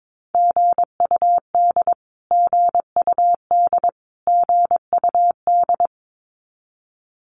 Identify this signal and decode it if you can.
Morse code, heard as a keyed tone, about 22 wpm, 700 Hz. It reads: GUD GUD GUD
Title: Abbreviations Repeated 3x - 22wpm